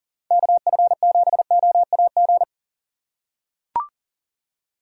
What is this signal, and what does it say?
Morse code, heard as a keyed tone, about 40 wpm, 700 Hz. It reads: KF7OAG
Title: USA Call Signs - 6 Characters - 3x Character Spacing - 40wpm